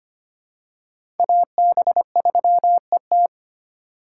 Morse code, heard as a keyed tone, about 25 wpm, 700 Hz. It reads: AB3ET
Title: USA Call Signs - 4x Character Spacing - 25wpm